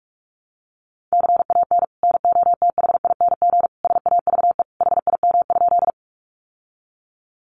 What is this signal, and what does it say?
Morse code, heard as a keyed tone, about 44 wpm, 700 Hz. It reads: CAN NOTHING SAVE HIM?
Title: Sentences from Top 1100 Words 45wpm